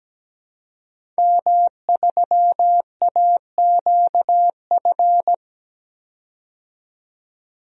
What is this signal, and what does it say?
Morse code, heard as a keyed tone, about 17 wpm, 700 Hz. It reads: M3AQF
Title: DXCC-100 Call Signs - 5 Characters - 17wpm